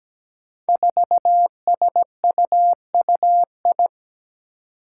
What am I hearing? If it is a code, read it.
Morse code, heard as a keyed tone, about 17 wpm, 700 Hz. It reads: 4SUUI
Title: Five Challenging Letter-Number Combinations 17wpm